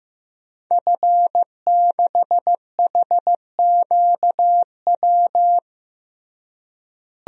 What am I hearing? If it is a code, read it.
Morse code, heard as a keyed tone, about 15 wpm, 700 Hz. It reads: F6HQW
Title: DXCC-100 Call Signs - 5 Characters - Repeated 2x - 15wpm